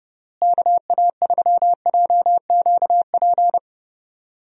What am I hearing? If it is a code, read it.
Morse code, heard as a keyed tone, about 30 wpm, 700 Hz. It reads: KA3JQP
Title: USA Call Signs - 6 Characters - Repeated 3x - 30wpm